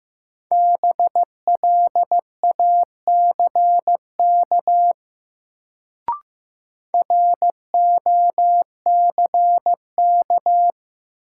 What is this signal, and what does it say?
Morse code, heard as a keyed tone, about 15 wpm, 700 Hz. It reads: BLACK ROCK
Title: Sentences from Top 400 Words - New Word Drill - 15wpm